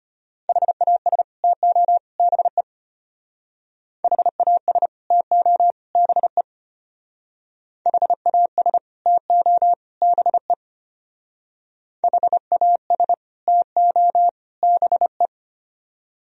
Morse code, heard as a keyed tone, about 30 wpm, 700 Hz. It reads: HAS TO BE HAS TO BE HAS TO BE HAS TO BE
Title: Speed-Racing - Sets of 3 Words 25wpm